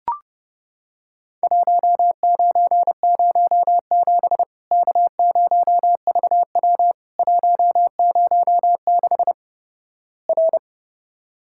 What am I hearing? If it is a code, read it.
Morse code, heard as a keyed tone, about 32 wpm, 700 Hz. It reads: 1907 K0VW 106 R
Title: Worked All Europe DX Contest - QTC Exchange - 30wpm